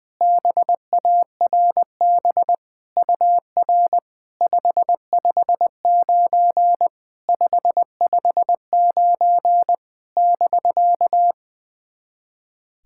Morse code, heard as a keyed tone, about 20 wpm, 700 Hz. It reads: BARB UR 559 559 <BK>
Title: SOTA QSO Practice 20wpm